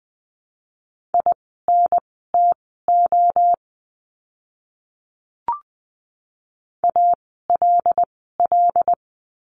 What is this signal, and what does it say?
Morse code, heard as a keyed tone, about 20 wpm, spaced as for 15 wpm, 700 Hz. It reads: INTO ALL
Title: 237 Lesson 38f - Letter q - Words 20wpm